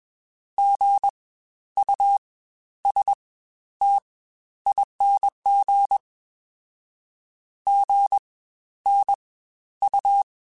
Morse code, heard as a keyed tone, about 20 wpm, spaced as for 11 wpm, 800 Hz. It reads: GUSTING GNU
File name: Spaced Alphas